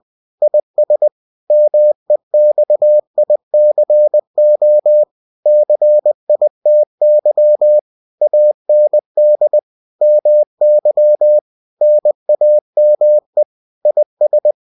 Morse code, heard as a keyed tone, about 20 wpm, 600 Hz. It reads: IS MEXICO CITY AND MY NAME IS